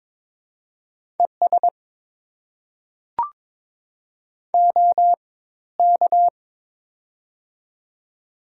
Morse code, heard as a keyed tone, about 22 wpm, spaced as for 14 wpm, 700 Hz. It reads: ES OK
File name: Top 90 QSO Elements - 4x Character Spacing - 22wpm